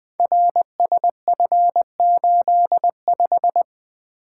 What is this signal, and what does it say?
Morse code, heard as a keyed tone, about 20 wpm, 700 Hz. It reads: RSF85
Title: Code Groups - 5 Letter:Numbers - 20wpm